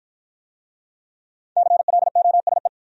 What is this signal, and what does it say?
Morse code, heard as a keyed tone, about 50 wpm, 700 Hz. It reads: CLOSE